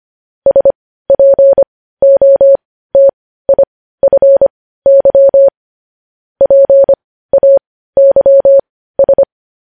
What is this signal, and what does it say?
Morse code, heard as a keyed tone, about 25 wpm, spaced as for 15 wpm, 550 Hz. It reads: SPOTIFY PAYS
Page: news-headlines-in-morse-code-at-15-wpm